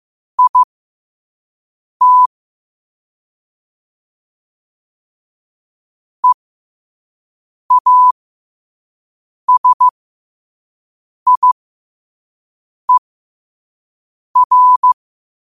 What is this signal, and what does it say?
Morse code, heard as a keyed tone, about 15 wpm, spaced as for 5 wpm, 1000 Hz. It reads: IT EASIER
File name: qotdfiles / 5wpm